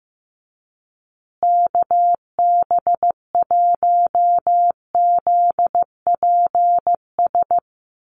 Morse code, heard as a keyed tone, about 15 wpm, 700 Hz. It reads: KB1ZPS
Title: USA Call Signs Repeated 2x 15wpm